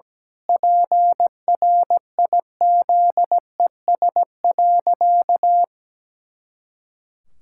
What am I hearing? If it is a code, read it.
Morse code, heard as a keyed tone, about 17 wpm, 700 Hz. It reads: PRIZES.
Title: Pangram Challenge v2 - 17wpm